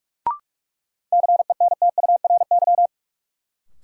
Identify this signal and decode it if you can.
Morse code, heard as a keyed tone, about 46 wpm, 700 Hz. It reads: CENTURY
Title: Sets of 5 Words - New Word Drill - 45wpm